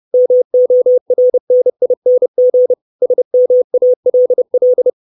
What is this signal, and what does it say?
Morse code, heard as a keyed tone, about 30 wpm, 500 Hz. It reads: MORNING SMALL
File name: 632 Words 30WPM